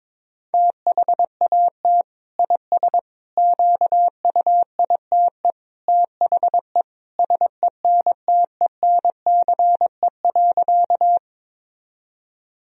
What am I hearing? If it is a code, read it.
Morse code, heard as a keyed tone, about 22 wpm, 700 Hz. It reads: THAT IS QUITE THE SENTENCE.
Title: Sentences from Top 1000 Words - Review Part 1 - 22wpm